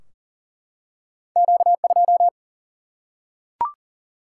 Morse code, heard as a keyed tone, about 38 wpm, 700 Hz. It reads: Q 2
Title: Single-Letter Single-Number 40wpm